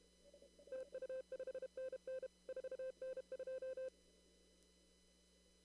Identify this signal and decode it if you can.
Morse code, heard as a keyed tone, about 31 wpm, 520 Hz. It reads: TU 5NN 4N2
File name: CW / kt5j wpx cw-op-k5pi